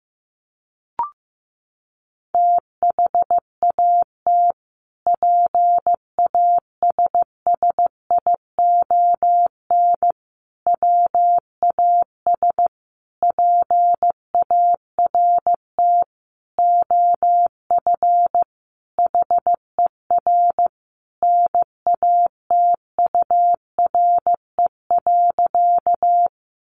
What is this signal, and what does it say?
Morse code, heard as a keyed tone, about 15 wpm, 700 Hz. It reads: THAT PASSION WAS PART OF HER NATURE.
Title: Sentences from Top 1300 Words 15wpm